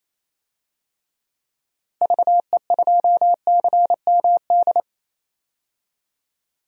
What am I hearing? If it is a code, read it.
Morse code, heard as a keyed tone, about 28 wpm, 700 Hz. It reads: VE2CMD
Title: DXCC-100 Call Signs - 6 Characters - Repeated 3x - 28wpm